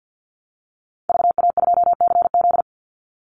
Morse code, heard as a keyed tone, about 48 wpm, 700 Hz. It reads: VA2CZ
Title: Call Signs Repeated 3x - 50wpm